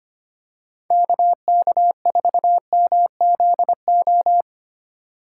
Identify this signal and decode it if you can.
Morse code, heard as a keyed tone, about 25 wpm, 700 Hz. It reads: KK4MZO
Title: USA Call Signs - 6 Characters - 3x Character Spacing - 25wpm